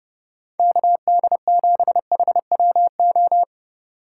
Morse code, heard as a keyed tone, about 36 wpm, 700 Hz. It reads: KD7HWO
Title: Speed Racing - USA 2x3 Call Signs 30wpm